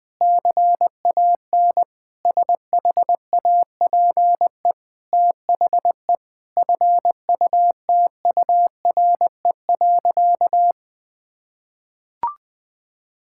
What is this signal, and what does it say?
Morse code, heard as a keyed tone, about 21 wpm, 700 Hz. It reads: CAN SHAPE THE FUTURE.
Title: Sentences from Top 1000 Words - Review Part 6 - 20wpm